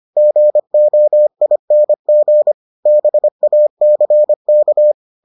Morse code, heard as a keyed tone, about 25 wpm, 600 Hz. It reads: GOING BACK